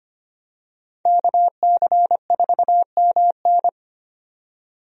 Morse code, heard as a keyed tone, about 25 wpm, 700 Hz. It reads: KC4MN
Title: USA Call Signs - 5 Characters - 25wpm